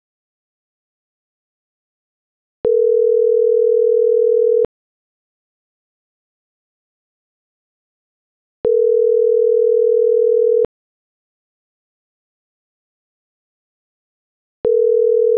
Ringback tone (North America), 440 + 480 Hz.